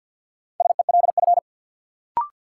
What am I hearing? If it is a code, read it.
Morse code, heard as a keyed tone, about 46 wpm, 700 Hz. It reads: SELF
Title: Sentences from Top 1000 Words - Review Part 1 - New Word Drill - 50wpm